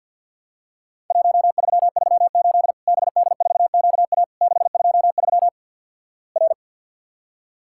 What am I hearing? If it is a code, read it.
Morse code, heard as a keyed tone, about 48 wpm, 700 Hz. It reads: 1338 BD4QA 623 R
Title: Worked All Europe DX Contest - QTC Exchange with Repeats - 50wpm